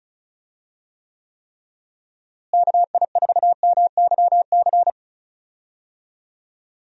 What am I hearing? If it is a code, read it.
Morse code, heard as a keyed tone, about 34 wpm, 700 Hz. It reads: KI4MYC
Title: USA Call Signs - 6 Characters - Repeated 3x - 35wpm